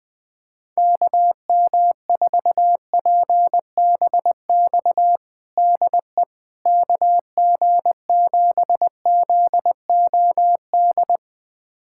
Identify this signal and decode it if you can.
Morse code, heard as a keyed tone, about 20 wpm, 700 Hz. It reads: KM4PBX DE KG7ZOD
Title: USA Callsign DE Callsign - 6 Characters - 20wpm